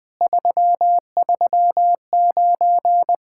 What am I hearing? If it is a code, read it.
Morse code, heard as a keyed tone, about 20 wpm, 700 Hz. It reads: 339
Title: Speed-Racing Common QSO Phrases - 20wpm